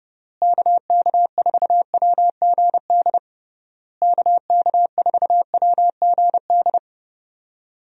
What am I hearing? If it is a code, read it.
Morse code, heard as a keyed tone, about 32 wpm, 700 Hz. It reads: KK4WGD KK4WGD
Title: USA Call Signs - 6 Characters - Repeated 3x - 30wpm